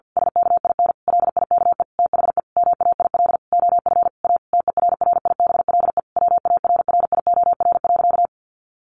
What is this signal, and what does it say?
Morse code, heard as a keyed tone, about 50 wpm, 700 Hz. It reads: SKIN LIKE THE MAIL OF A TERRIBLE WARRIOR.
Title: A Sound Of Thunder 50wpm